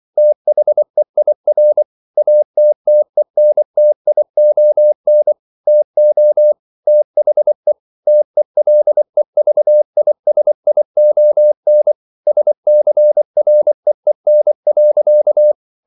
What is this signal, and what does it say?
Morse code, heard as a keyed tone, about 24 wpm, 600 Hz. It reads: THEIR ATTENTION TO THE TELEVISION SCREEN.